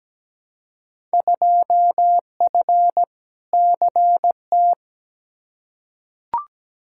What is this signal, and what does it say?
Morse code, heard as a keyed tone, about 17 wpm, 700 Hz. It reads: 2F CT
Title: Field Day Exchanges Encore - 17wpm